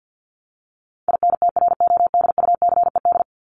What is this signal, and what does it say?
Morse code, heard as a keyed tone, about 50 wpm, 700 Hz. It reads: INTRODUCED
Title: ED Suffix - Top 100 Words with Base Words 50wpm